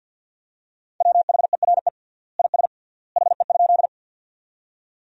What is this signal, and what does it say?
Morse code, heard as a keyed tone, about 53 wpm, 700 Hz. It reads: WHERE IS HE?
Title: 3x Word-Spacing - Sentences from Top 200 Words - 50wpm